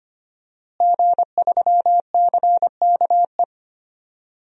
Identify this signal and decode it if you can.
Morse code, heard as a keyed tone, about 25 wpm, 700 Hz. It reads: G3CKE
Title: DXCC-100 Call Signs - 5 Characters - Repeated 2x - 25wpm